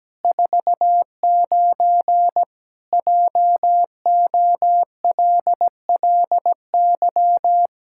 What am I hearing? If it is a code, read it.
Morse code, heard as a keyed tone, about 17 wpm, 700 Hz. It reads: 49 JOLLY
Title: Pangram Challenge v2 - 17wpm